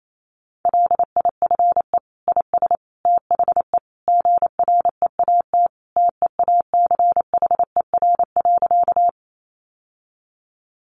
Morse code, heard as a keyed tone, about 28 wpm, 700 Hz. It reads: LIFE IS THE GREAT TEACHER.
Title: Sentences from Top 900 Words 28wpm